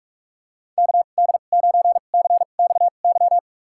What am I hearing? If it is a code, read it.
Morse code, heard as a keyed tone, about 45 wpm, spaced as for 32 wpm, 700 Hz. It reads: KD9CXY
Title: USA Call Signs - 2x Character Spacing - 45wpm